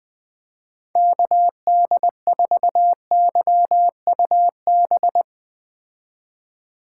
Morse code, heard as a keyed tone, about 20 wpm, 700 Hz. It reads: KD4YUB
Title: Speed Racing - USA 2x3 Call Signs 17wpm